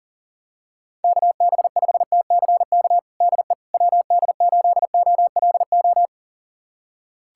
Morse code, heard as a keyed tone, about 40 wpm, 700 Hz. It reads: KB5TCK DE WD8OLO
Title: USA Callsign DE Callsign - 6 Characters - 40wpm